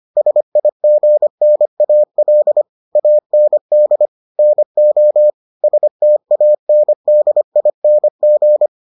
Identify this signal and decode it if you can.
Morse code, heard as a keyed tone, about 25 wpm, 600 Hz. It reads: SIGNAL AND NO STANDING